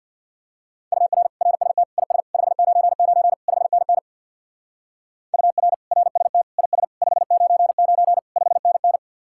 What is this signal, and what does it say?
Morse code, heard as a keyed tone, about 54 wpm, 700 Hz. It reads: UR RST IS 599 5NN UR RST IS 599 5NN
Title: Speed-Racing Common QSO Phrases - 50wpm